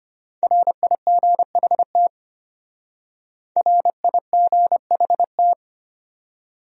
Morse code, heard as a keyed tone, about 32 wpm, 700 Hz. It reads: RIGHT RIGHT